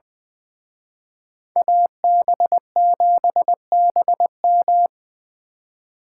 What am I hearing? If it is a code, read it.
Morse code, heard as a keyed tone, about 24 wpm, 700 Hz. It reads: AB7BM
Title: Speed Racing - USA 2x2 Call Signs 20wpm